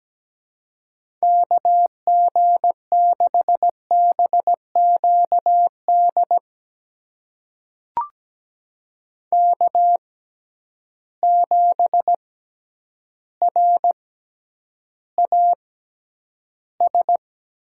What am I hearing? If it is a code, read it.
Morse code, heard as a keyed tone, about 17 wpm, spaced as for 9 wpm, 700 Hz. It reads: KG6BQD K7RAS
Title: USA Call Signs - 6x Character Spacing - 17wpm